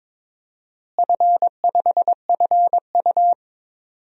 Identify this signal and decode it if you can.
Morse code, heard as a keyed tone, about 22 wpm, 700 Hz. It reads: F5FU
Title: DXCC-100 Call Signs - 4 Characters - Repeated 3x - 22wpm